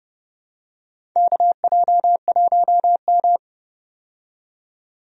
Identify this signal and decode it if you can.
Morse code, heard as a keyed tone, about 30 wpm, 700 Hz. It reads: KJ1M
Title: USA Call Signs - 4 Characters - 30wpm